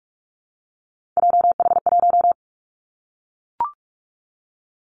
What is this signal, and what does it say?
Morse code, heard as a keyed tone, about 44 wpm, 700 Hz. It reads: JH1
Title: Callsign Prefixes for Top 25 DXCC - Repeated 3x - 45wpm